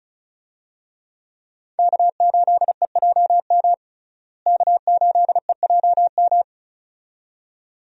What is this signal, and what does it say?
Morse code, heard as a keyed tone, about 36 wpm, 700 Hz. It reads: K8EJM K8EJM
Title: USA Call Signs - 5 Characters - Repeated 2x - 35wpm